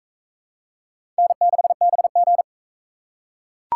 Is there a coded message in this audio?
Yes: Morse code from 1.2 to 2.4 s, heard as a keyed tone, about 36 wpm, 700 Hz. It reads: N6BG